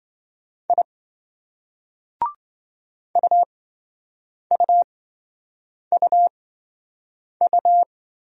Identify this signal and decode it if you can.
Morse code, heard as a keyed tone, about 26 wpm, spaced as for 7 wpm, 700 Hz. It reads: I UUUU